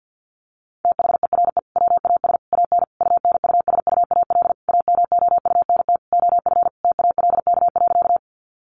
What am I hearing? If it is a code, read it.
Morse code, heard as a keyed tone, about 50 wpm, 700 Hz. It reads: THERE WAS AN UNUSUAL AMOUNT OF TALK.
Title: Sentences from Top 1400 Words 50wpm